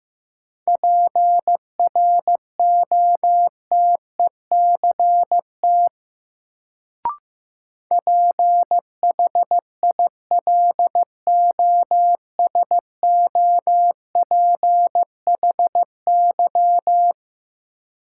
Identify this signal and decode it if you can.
Morse code, heard as a keyed tone, about 15 wpm, 700 Hz. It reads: PROTECT PHILOSOPHY
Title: Sentences from Top 900 Words - New Word Drill - 15wpm